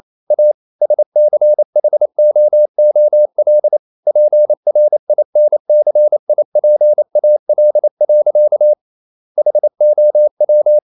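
Morse code, heard as a keyed tone, about 28 wpm, 600 Hz. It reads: A SCHOOL PRINCIPAL. HOW
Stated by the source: Morse code practice at 28 wpm, text of Ken Follett's novel Never in English, from CW med Gnister.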